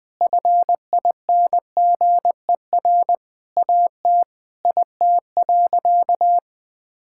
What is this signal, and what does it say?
Morse code, heard as a keyed tone, about 20 wpm, 700 Hz. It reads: FINGER AT IT.
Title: Sentences from Top 1600 Words 20wpm